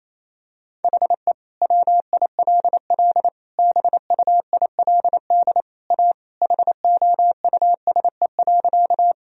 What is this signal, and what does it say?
Morse code, heard as a keyed tone, about 27 wpm, 700 Hz. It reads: HE WILL BUILD A HOUSE.
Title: Sentences from Top 1000 Words - Review Part 1 - 28wpm